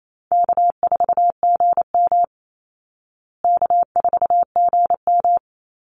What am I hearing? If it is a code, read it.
Morse code, heard as a keyed tone, about 28 wpm, 700 Hz. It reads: K4GM K4GM
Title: Easy Call Signs Repeated 3x - 28wpm